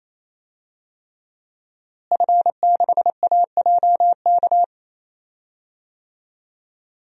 Morse code, heard as a keyed tone, about 28 wpm, 700 Hz. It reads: F6AJK
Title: DXCC-100 Call Signs - 5 Characters - Repeated 3x - 28wpm